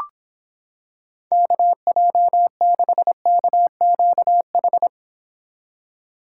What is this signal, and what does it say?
Morse code, heard as a keyed tone, about 26 wpm, 700 Hz. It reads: KJ6KQH